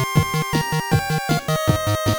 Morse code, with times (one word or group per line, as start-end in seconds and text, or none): none